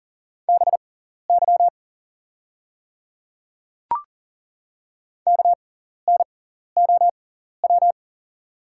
Morse code, heard as a keyed tone, about 40 wpm, spaced as for 14 wpm, 700 Hz. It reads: BY KNOW
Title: 6x Character Spacing - Top 100 Words - 40wpm